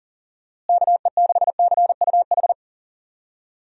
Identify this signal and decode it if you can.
Morse code, heard as a keyed tone, about 40 wpm, 700 Hz. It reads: KE6CUH
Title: USA Call Signs - 6 Characters - Repeated 3x - 40wpm